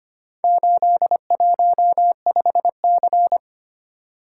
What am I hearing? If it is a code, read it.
Morse code, heard as a keyed tone, about 25 wpm, 700 Hz. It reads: 815C
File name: practice / 25wpm / Four Letter-Number Combinations - 25wpm